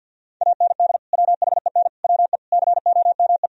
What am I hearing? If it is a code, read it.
Morse code, heard as a keyed tone, about 65 wpm, spaced as for 57 wpm, 700 Hz. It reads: AND WHEN WE COME
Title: Speed-Racing - Sets of 5 Words 50wpm